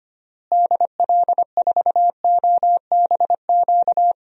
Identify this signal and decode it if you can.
Morse code, heard as a keyed tone, about 25 wpm, 700 Hz. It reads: DL4OBQ
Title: USA Call Signs - 6 Characters - Repeated 3x - 25wpm